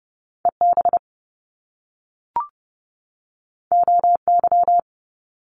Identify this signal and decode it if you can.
Morse code, heard as a keyed tone, about 30 wpm, 700 Hz. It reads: EB OY
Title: Top Two Letter Combinations in Words 30wpm